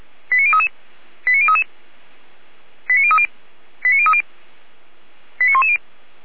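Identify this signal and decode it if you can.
Five-tone selective call (ZVEI), heard as quick runs of five tones, tones of about 70 ms. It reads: 89033 89033 89030 89030 89110
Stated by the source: ZVEI five-tone selective call 89401 received by SDR.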